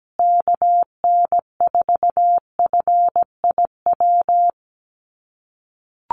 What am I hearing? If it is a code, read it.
Morse code, heard as a keyed tone, about 17 wpm, 700 Hz. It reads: KN4FIW
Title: USA Call Signs Repeated 2x 17wpm